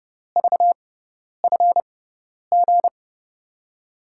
Morse code, heard as a keyed tone, about 30 wpm, spaced as for 10 wpm, 700 Hz. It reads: VFG
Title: USA Call Signs - 6x Character Spacing - 30wpm